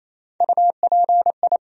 Morse code, heard as a keyed tone, about 28 wpm, 700 Hz. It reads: UPI
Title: USA Call Sign Postfixes - 3 Letters 28wpm